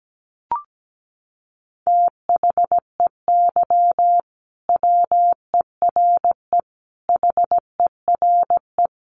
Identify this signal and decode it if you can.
Morse code, heard as a keyed tone, about 17 wpm, 700 Hz. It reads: THEY WERE HERE
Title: Sentences from Top 1200 Words 17wpm